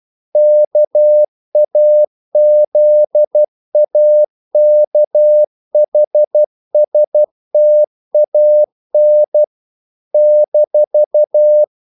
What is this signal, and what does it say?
Morse code, heard as a keyed tone, about 12 wpm, 600 Hz. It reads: KAZAKHSTAN -
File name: CW / Practice / NationalCapitals12